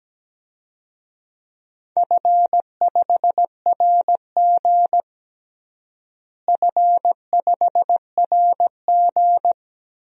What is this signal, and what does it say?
Morse code, heard as a keyed tone, about 17 wpm, 700 Hz. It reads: F5RG F5RG
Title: DXCC-100 Call Signs - 4 Characters - Repeated 3x - 17wpm